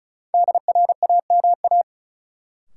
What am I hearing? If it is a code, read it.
Morse code, heard as a keyed tone, about 34 wpm, 700 Hz. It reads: DRAMA
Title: Sentences from Top 1500 Words - New Word Drill - 35wpm